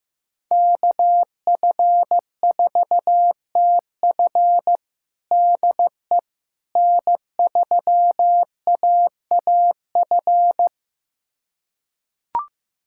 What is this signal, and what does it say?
Morse code, heard as a keyed tone, about 15 wpm, 700 Hz. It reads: KF4TF DE N3AAF
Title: USA Callsign DE Callsign - 5 Characters - 15wpm